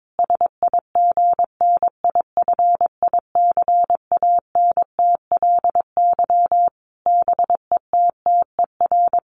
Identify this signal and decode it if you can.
Morse code, heard as a keyed tone, about 22 wpm, 700 Hz. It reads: SIGNIFICANTLY BETTER
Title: Sentences from Top 1500 Words 22wpm